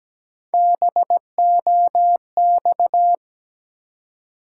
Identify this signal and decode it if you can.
Morse code, heard as a keyed tone, about 17 wpm, 700 Hz. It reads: BOX
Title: Three Letter Words - 17wpm